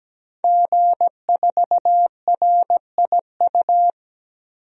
Morse code, heard as a keyed tone, about 17 wpm, 700 Hz. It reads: G4RIU
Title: DXCC-100 Call Signs - 5 Characters - 17wpm